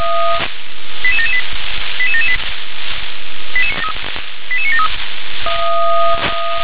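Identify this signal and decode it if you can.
Five-tone selective call (ZVEI), heard as quick runs of five tones, tones of about 80 ms. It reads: 88778 88778 88973
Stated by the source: ZVEI five-tone selective call received on SDR.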